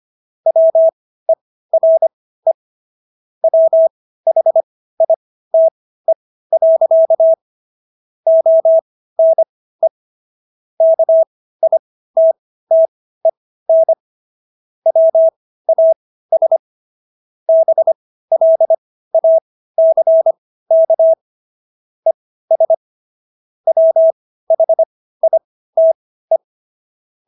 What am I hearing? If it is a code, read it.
Morse code, heard as a keyed tone, about 25 wpm, spaced as for 15 wpm, 650 Hz. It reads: WERE WHITE. ONE KITTEN WAS BLACK ES WHITE